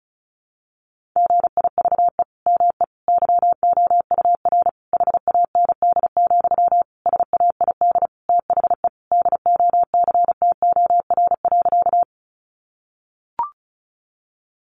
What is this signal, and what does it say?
Morse code, heard as a keyed tone, about 34 wpm, 700 Hz. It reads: GIVE ME YOUR HAND, SAID THE DOCTOR.
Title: Sentences from Top 1500 Words 35wpm